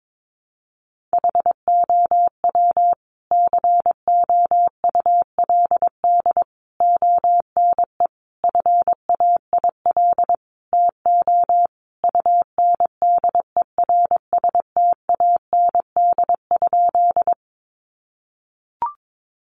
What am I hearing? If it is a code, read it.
Morse code, heard as a keyed tone, about 22 wpm, 700 Hz. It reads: HOW COULD ONE FAIL TO UNDERSTAND?
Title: Sentences from Top 1100 Words 22wpm